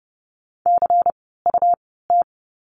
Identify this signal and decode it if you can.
Morse code, heard as a keyed tone, about 30 wpm, spaced as for 17 wpm, 700 Hz. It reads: CUT
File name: Lesson 17f - Letter u - Words 30wpm